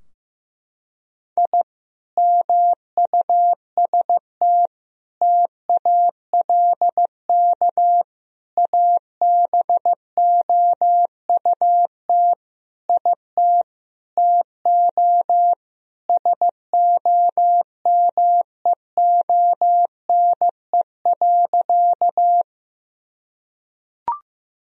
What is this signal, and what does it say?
Morse code, heard as a keyed tone, about 15 wpm, 700 Hz. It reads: I MUST TALK ABOUT IT TO SOMEONE.
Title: Sentences from Top 1000 Words - Review Part 4 - 15wpm